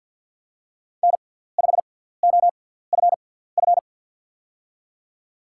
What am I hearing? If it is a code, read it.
Morse code, heard as a keyed tone, about 53 wpm, spaced as for 17 wpm, 700 Hz. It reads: N5OFF
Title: USA Call Signs - 6x Character Spacing - 50wpm